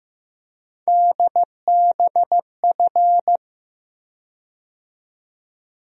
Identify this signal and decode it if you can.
Morse code, heard as a keyed tone, about 15 wpm, 700 Hz. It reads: DBF
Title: USA Call Sign Postfixes - 3 Letters 15wpm